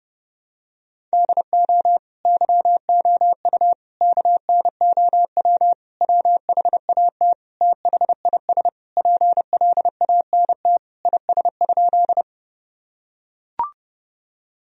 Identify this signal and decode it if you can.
Morse code, heard as a keyed tone, about 32 wpm, 700 Hz. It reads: DO YOU KNOW WHAT THIS PLANT IS?
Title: Sentences from Top 1000 Words - Review Part 1 - 30wpm